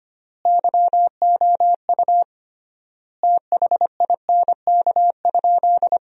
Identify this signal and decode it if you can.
Morse code, heard as a keyed tone, about 25 wpm, 700 Hz. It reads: YOU THINK?
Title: 3x Word-Spacing - Sentences from Top 200 Words - 25wpm